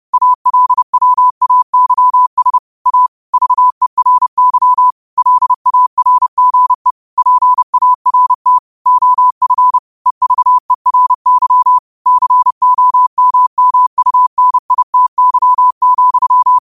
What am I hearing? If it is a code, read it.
Morse code, heard as a keyed tone, about 30 wpm, 1000 Hz. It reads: ALWAYS A VERY LARGE PART OF EVERY COMMUNITY,
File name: qotdfiles / 30wpm